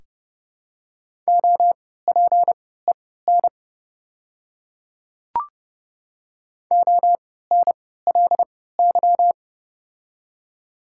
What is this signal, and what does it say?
Morse code, heard as a keyed tone, about 32 wpm, spaced as for 17 wpm, 700 Hz. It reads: OPEN ONLY
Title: 3x Character Spacing - Top 300 Words - 30wpm